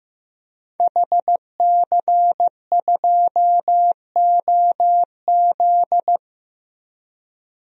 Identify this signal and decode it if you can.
Morse code, heard as a keyed tone, about 15 wpm, 700 Hz. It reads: HC2OZ
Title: DXCC-100 Call Signs - 5 Characters - 15wpm